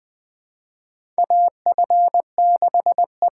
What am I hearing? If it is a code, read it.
Morse code, heard as a keyed tone, about 20 wpm, 700 Hz. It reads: AF6E
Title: USA Call Signs - 5 Characters - 6x Character Spacing - 20wpm